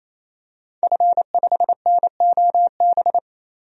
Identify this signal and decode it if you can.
Morse code, heard as a keyed tone, about 28 wpm, 700 Hz. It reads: F5NOB
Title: DXCC-100 Call Signs - 5 Characters - 28wpm